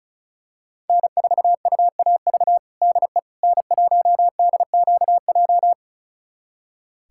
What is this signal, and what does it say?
Morse code, heard as a keyed tone, about 34 wpm, 700 Hz. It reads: N4UAV DE N1DQJ
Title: USA Callsign DE Callsign - 5 Characters - 35wpm